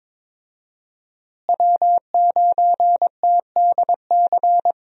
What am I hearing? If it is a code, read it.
Morse code, heard as a keyed tone, about 21 wpm, 700 Hz. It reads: W9TDC
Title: USA Call Signs - 5 Characters - Repeated 2x - 22wpm